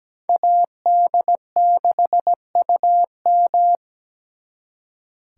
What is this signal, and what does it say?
Morse code, heard as a keyed tone, about 20 wpm, 700 Hz. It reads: AD6UM
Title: Speed Racing - USA 2x2 Call Signs 17wpm